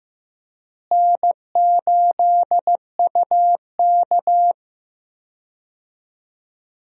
Morse code, heard as a keyed tone, about 15 wpm, 700 Hz. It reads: N8UK
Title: USA Call Signs - 4 Characters - Repeated 3x - 15wpm